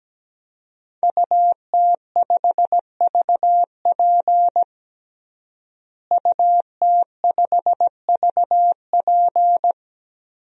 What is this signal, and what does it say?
Morse code, heard as a keyed tone, about 17 wpm, 700 Hz. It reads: UT5VP UT5VP
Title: DXCC-100 Call Signs - 5 Characters - Repeated 3x - 17wpm